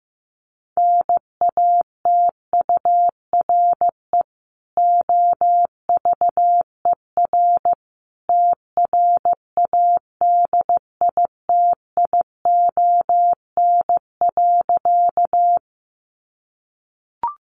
Morse code, heard as a keyed tone, about 14 wpm, 700 Hz. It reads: NATURE OVER TRADITION.
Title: Sentences from Top 1500 Words 15wpm